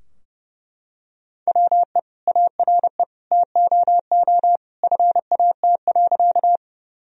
Morse code, heard as a keyed tone, about 32 wpm, 700 Hz. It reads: WE ARE TOO FAT.
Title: Sentences from Top 1000 Words - Review Part 4 - 30wpm